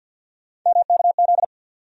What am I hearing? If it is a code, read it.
Morse code, heard as a keyed tone, about 50 wpm, 700 Hz. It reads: MKZ
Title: USA Call Sign Postfixes - 3 Letters 50wpm